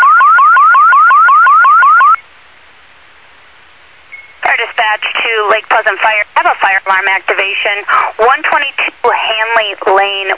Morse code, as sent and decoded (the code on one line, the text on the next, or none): none